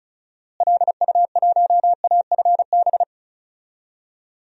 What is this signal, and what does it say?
Morse code, heard as a keyed tone, about 36 wpm, 700 Hz. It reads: LU1AFB
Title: DXCC-100 Call Signs - 6 Characters - Repeated 3x - 35wpm